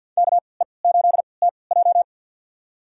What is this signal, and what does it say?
Morse code, heard as a keyed tone, about 52 wpm, spaced as for 28 wpm, 700 Hz. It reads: KE8TJ
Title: USA Call Signs - 5 Characters - 3x Character Spacing - 50wpm